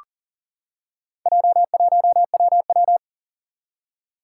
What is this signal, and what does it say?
Morse code, heard as a keyed tone, about 38 wpm, 700 Hz. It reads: J1WW
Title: Four Challenging Letter-Number Combinations 40wpm